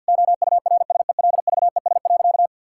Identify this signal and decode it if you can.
Morse code, heard as a keyed tone, about 50 wpm, 700 Hz. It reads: OURSELVES.
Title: Sentences from Top 1600 Words 50wpm